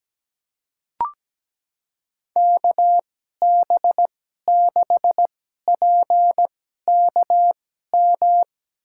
Morse code, heard as a keyed tone, about 17 wpm, spaced as for 12 wpm, 700 Hz. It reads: KB6PKM
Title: USA Call Signs - 2x Character Spacing - 17wpm